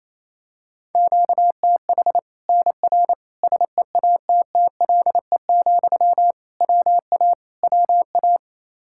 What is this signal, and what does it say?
Morse code, heard as a keyed tone, about 28 wpm, 700 Hz. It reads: QTH NR SEATTLE, WA WA
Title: Practice QSO - Exchange 1 - Beginning - 28wpm